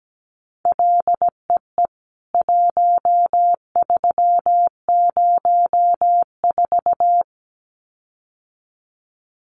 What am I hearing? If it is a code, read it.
Morse code, heard as a keyed tone, about 17 wpm, 700 Hz. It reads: LEE 1304
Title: CWT Exchange 17wpm